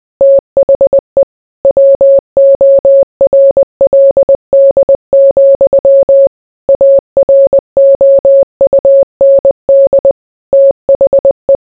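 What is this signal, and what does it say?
Morse code, heard as a keyed tone, about 20 wpm, 550 Hz. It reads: THE WORLD, AROUND THE
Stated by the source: Morse code mixed with noise and music as background sound for sleep.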